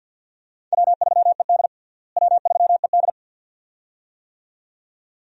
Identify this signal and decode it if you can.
Morse code, heard as a keyed tone, about 48 wpm, 700 Hz. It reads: W3ED W3ED
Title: USA Call Signs - 4 Characters - Repeated 2x - 50wpm